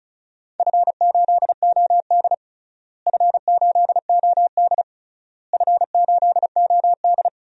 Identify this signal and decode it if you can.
Morse code, heard as a keyed tone, about 36 wpm, 700 Hz. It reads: F8OD F8OD F8OD
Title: DXCC-100 Call Signs - 4 Characters - Repeated 3x - 35wpm